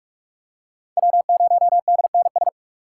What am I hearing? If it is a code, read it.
Morse code, heard as a keyed tone, about 44 wpm, 700 Hz. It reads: W0DNS
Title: USA Call Signs - 5 Characters - 6x Character Spacing - 45wpm